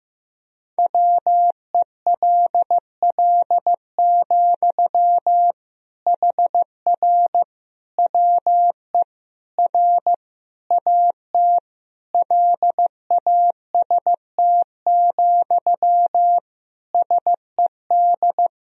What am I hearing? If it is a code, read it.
Morse code, heard as a keyed tone, about 15 wpm, 700 Hz. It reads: WELL, HR WE R AT LAST, SED
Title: Sentences from Top 300 Words and Abbreviations 15wpm